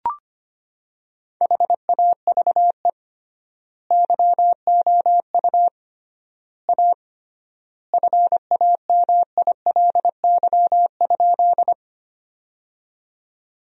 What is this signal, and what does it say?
Morse code, heard as a keyed tone, about 25 wpm, 700 Hz. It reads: HAVE YOU A FAMILY?
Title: 3x Word-Spacing - Sentences from Top 400 Words - 25wpm